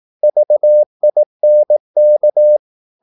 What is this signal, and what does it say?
Morse code, heard as a keyed tone, about 18 wpm, 600 Hz. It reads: VINK
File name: Korte ord DK 18wpm